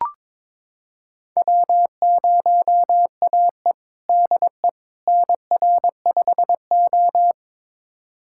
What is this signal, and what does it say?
Morse code, heard as a keyed tone, about 22 wpm, 700 Hz. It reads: W0AE DE NR5O
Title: USA Callsign DE Callsign - 4 Characters - 22wpm